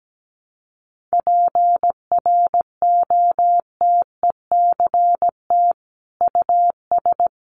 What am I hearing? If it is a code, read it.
Morse code, heard as a keyed tone, about 17 wpm, 700 Hz. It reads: PROTECT US
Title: Sentences from Top 900 Words 17wpm